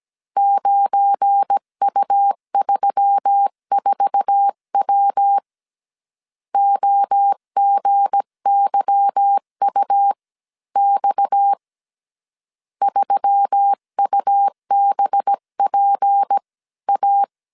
Morse code, heard as a keyed tone, about 17 wpm, spaced as for 15 wpm, 800 Hz. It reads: 9U34W OGYUX 3UBPA
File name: CW / DET / 1 9 1500C S B 16wpm 2856